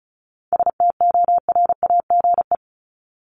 Morse code, heard as a keyed tone, about 34 wpm, 700 Hz. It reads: STORAGE
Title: Top 1251-1500 Words 35wpm